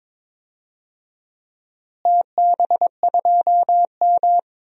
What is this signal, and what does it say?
Morse code, heard as a keyed tone, about 22 wpm, 700 Hz. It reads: TB2M
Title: DXCC-100 Call Signs - 4 Characters - Repeated 2x - 22wpm